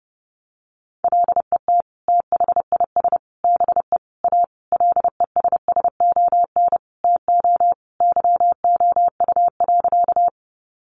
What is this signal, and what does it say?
Morse code, heard as a keyed tone, about 30 wpm, 700 Hz. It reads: LET THIS BE A LESSON TO YOU.
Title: Sentences from Top 1200 Words 30wpm